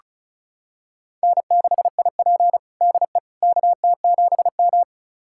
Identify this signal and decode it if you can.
Morse code, heard as a keyed tone, about 34 wpm, 700 Hz. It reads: N6IP DE KT7M
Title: USA Callsign DE Callsign - 4 Characters - 35wpm